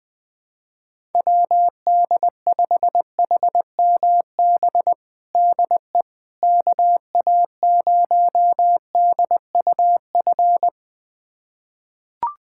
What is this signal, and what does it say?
Morse code, heard as a keyed tone, about 20 wpm, 700 Hz. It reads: WD5HMB DE KA0DUF
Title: USA Callsign DE Callsign 20wpm